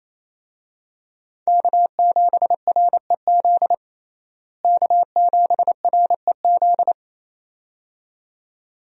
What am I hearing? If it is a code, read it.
Morse code, heard as a keyed tone, about 28 wpm, 700 Hz. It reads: K7REZ K7REZ
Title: USA Call Signs - 5 Characters - Repeated 2x - 28wpm